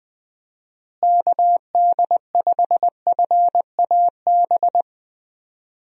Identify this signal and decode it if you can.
Morse code, heard as a keyed tone, about 20 wpm, 700 Hz. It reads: KD5FAB
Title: USA Call Signs - 6 Characters - Repeated 2x - 20wpm